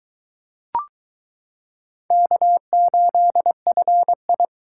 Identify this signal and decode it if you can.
Morse code, heard as a keyed tone, about 22 wpm, 700 Hz. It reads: K8FI